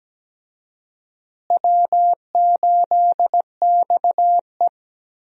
Morse code, heard as a keyed tone, about 17 wpm, 700 Hz. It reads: W8XE
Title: USA Call Signs - 4 Characters - Repeated 2x - 17wpm